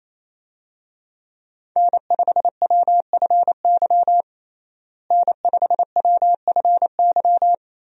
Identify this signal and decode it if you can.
Morse code, heard as a keyed tone, about 28 wpm, 700 Hz. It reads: N5WFY N5WFY
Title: USA Call Signs - 5 Characters - Repeated 3x - 28wpm